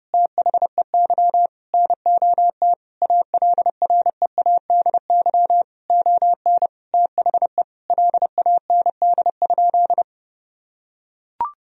Morse code, heard as a keyed tone, about 32 wpm, 700 Hz. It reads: THEY NOT ALREADY ON THE LAND?
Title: Sentences from Top 1000 Words - Review Part 1 - 30wpm